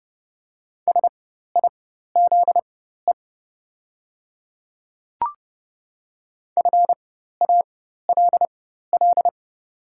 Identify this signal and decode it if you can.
Morse code, heard as a keyed tone, about 31 wpm, spaced as for 14 wpm, 690 Hz. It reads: SIZE FALL
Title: 4x Character Spacing - Top 500 Words - 30wpm